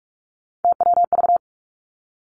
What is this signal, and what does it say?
Morse code, heard as a keyed tone, about 46 wpm, 700 Hz. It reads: TWV
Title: Three Challenging Letter-Number Combinations 45wpm